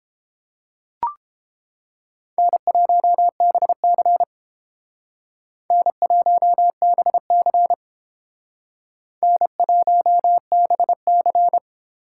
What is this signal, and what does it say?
Morse code, heard as a keyed tone, about 29 wpm, 700 Hz. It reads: N1BC N1BC N1BC